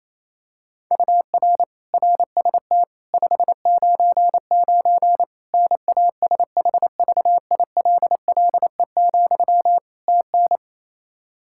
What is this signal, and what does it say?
Morse code, heard as a keyed tone, about 28 wpm, 700 Hz. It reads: UR RST 599 NASHVILLE, TN
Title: Practice QSO - Exchange 1 - Beginning - 28wpm